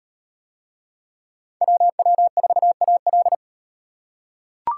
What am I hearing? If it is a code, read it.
Morse code, heard as a keyed tone, about 35 wpm, 700 Hz. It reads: WW4AL